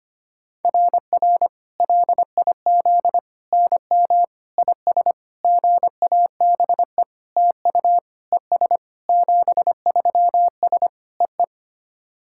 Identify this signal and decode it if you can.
Morse code, heard as a keyed tone, about 25 wpm, 700 Hz. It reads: RR LIZ NM IS GABE TU ES 73S EE
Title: SOTA QSO Practice 25wpm